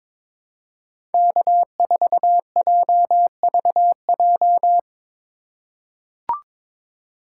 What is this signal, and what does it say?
Morse code, heard as a keyed tone, about 22 wpm, 700 Hz. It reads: K4JVJ
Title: USA Call Signs - 5 Characters - 3x Character Spacing - 22wpm